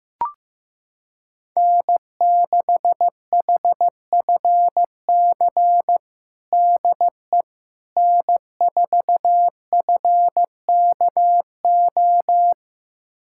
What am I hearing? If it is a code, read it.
Morse code, heard as a keyed tone, about 15 wpm, 700 Hz. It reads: N6HFC DE N4FKO
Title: USA Callsign DE Callsign - 5 Characters - 15wpm